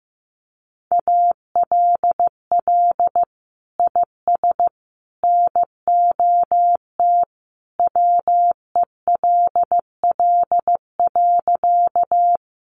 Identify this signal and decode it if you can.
Morse code, heard as a keyed tone, about 15 wpm, 700 Hz. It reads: ALL IS NOT WELL.
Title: Sentences from Top 600 Words 15wpm